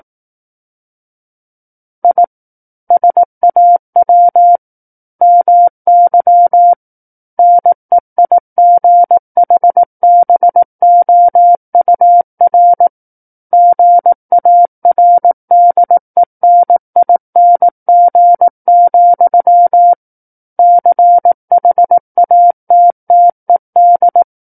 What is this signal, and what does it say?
Morse code, heard as a keyed tone, about 18 wpm, 700 Hz. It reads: I SAW MY NEIGHBOUR GARDENING, CHATTED
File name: 18 WPM